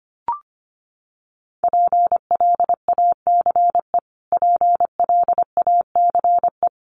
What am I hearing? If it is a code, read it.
Morse code, heard as a keyed tone, about 25 wpm, 700 Hz. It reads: PLACE PLACE
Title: Top 300 Words Repeated 25wpm